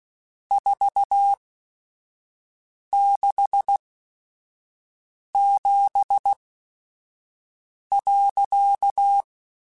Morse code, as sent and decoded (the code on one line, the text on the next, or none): ....- -.... --... .-.-.-
467.